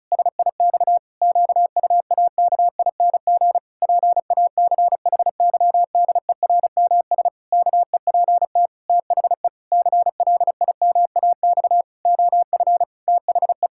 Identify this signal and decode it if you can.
Morse code, heard as a keyed tone, about 34 wpm, 700 Hz. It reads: SIX QUAKING PACHYDERMS KEPT THE CLIMAX OF THE
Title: Pangram Challenge v2 - 35wpm